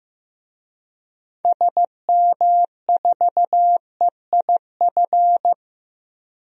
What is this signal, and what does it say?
Morse code, heard as a keyed tone, about 15 wpm, 700 Hz. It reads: SM4EIF
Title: DXCC-100 Call Signs - 6 Characters - Repeated 3x - 15wpm